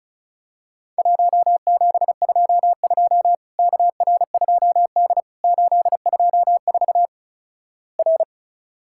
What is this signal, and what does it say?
Morse code, heard as a keyed tone, about 36 wpm, 700 Hz. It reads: 1722 KR2D 824 R
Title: Worked All Europe DX Contest - QTC Exchange - 35wpm